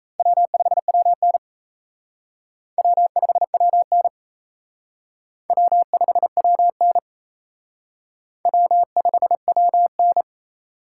Morse code, heard as a keyed tone, about 35 wpm, 700 Hz. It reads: W5WN W5WN W5WN W5WN